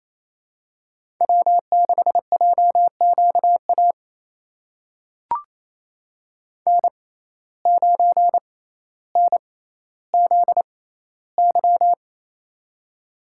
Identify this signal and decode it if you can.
Morse code, heard as a keyed tone, about 28 wpm, spaced as for 14 wpm, 700 Hz. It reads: W6JQA N9NZY
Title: USA Call Signs - 5 Characters - 6x Character Spacing - 28wpm